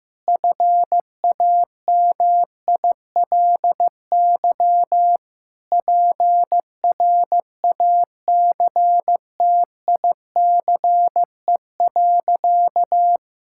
Morse code, heard as a keyed tone, about 16 wpm, 700 Hz. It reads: FAMILY PRACTICE.
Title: Sentences from Top 1000 Words - Review Part 5 - 15wpm